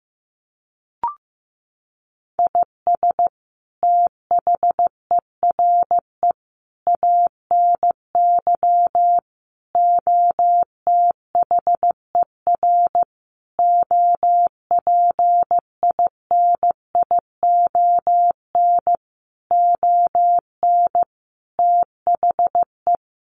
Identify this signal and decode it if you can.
Morse code, heard as a keyed tone, about 14 wpm, 700 Hz. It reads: IS THERE ANY OTHER OPINION ON THE
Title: Sentences from Top 1200 Words 15wpm